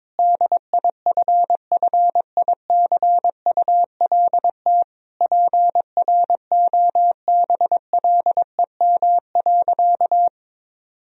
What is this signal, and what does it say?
Morse code, heard as a keyed tone, about 22 wpm, 700 Hz. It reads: DIFFICULT PROBLEM.
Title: Sentences from Top 1000 Words - Review Part 2 - 22wpm